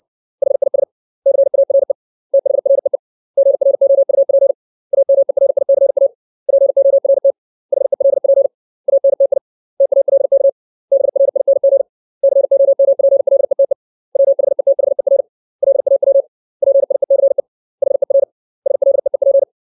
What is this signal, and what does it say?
Morse code, heard as a keyed tone, about 60 wpm, 550 Hz. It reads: HIS CARE THREE GROUP AMERICA PORT HELP ANTI TALK BRING COMPLETE WHETHER LAW PIECE HER SLEEP
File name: 632 words / 632 Words 60WPM